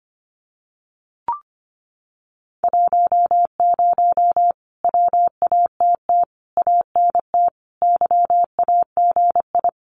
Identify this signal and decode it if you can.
Morse code, heard as a keyed tone, about 25 wpm, 700 Hz. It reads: 10 WATT ANT YAGI
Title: Common QSO Phrases Encore 25wpm